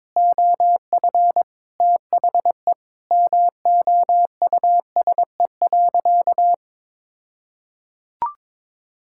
Morse code, heard as a keyed tone, about 22 wpm, 700 Hz. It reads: OF THE MOUSE.
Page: morse-code-ninja-practice-22wpm